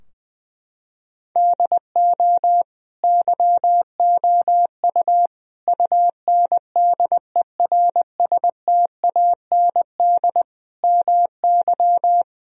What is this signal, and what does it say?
Morse code, heard as a keyed tone, about 20 wpm, 700 Hz. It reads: DO YOU UNDERSTAND MY
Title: Sentences from Top 1000 Words - Review Part 3 - 20wpm